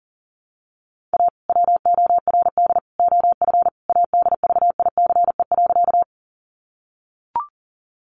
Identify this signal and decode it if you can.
Morse code, heard as a keyed tone, about 40 wpm, 700 Hz. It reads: A WORD OF ADVICE.
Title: Sentences from Top 1100 Words 40wpm